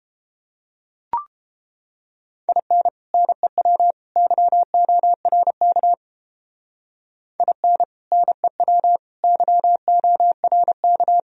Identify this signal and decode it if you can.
Morse code, heard as a keyed tone, about 30 wpm, 700 Hz. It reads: IN NEW YORK IN NEW YORK